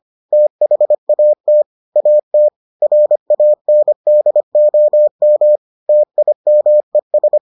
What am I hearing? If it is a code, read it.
Morse code, heard as a keyed tone, about 25 wpm, 600 Hz. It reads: THAT AT RANDOM TIMES